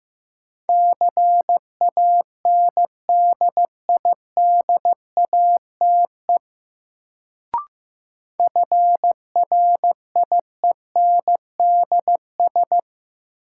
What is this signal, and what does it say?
Morse code, heard as a keyed tone, about 15 wpm, 700 Hz. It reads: CANDIDATE FRIENDS
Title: Sentences from Top 1600 Words - New Word Drill Alt - 15wpm